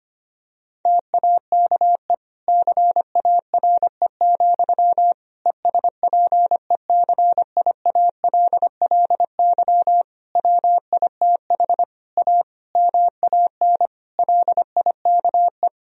Morse code, heard as a keyed tone, about 25 wpm, 700 Hz. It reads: TAKE CARE, ESPECIALLY WITH A MAN LIKE
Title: Sentences from Top 1000 Words - Review Part 4 - 25wpm